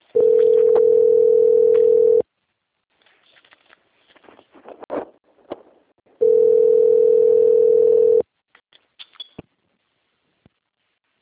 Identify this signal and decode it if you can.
Ringback tone (North America), 440 + 480 Hz.
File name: out- - - - - .5093